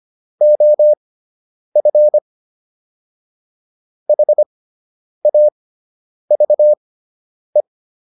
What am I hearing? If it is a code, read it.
Morse code, heard as a keyed tone, about 25 wpm, spaced as for 9 wpm, 600 Hz. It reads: OF HAVE